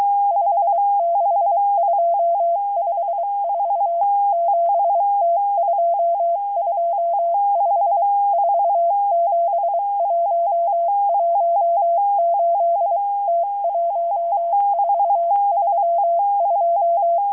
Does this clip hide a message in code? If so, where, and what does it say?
Yes: Morse code from 0.3 to 17.2 s, heard as a keyed tone, about 23 wpm, 680 Hz. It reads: 56254 7T225 47118 T1432